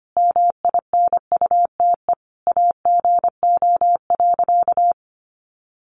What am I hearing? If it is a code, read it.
Morse code, heard as a keyed tone, about 25 wpm, 700 Hz. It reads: MINUTE AGO.
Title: Sentences from Top 500 Words 25wpm